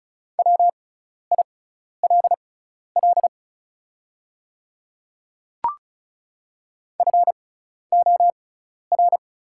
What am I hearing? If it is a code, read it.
Morse code, heard as a keyed tone, about 36 wpm, spaced as for 12 wpm, 700 Hz. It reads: WILL FOR
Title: 6x Character Spacing - Top 100 Words - 35wpm